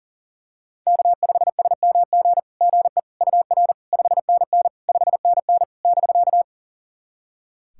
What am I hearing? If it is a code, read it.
Morse code, heard as a keyed tone, about 40 wpm, 700 Hz. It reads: K5SMG GE UR 5NN 5NN <BK>
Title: SOTA QSO Practice 40wpm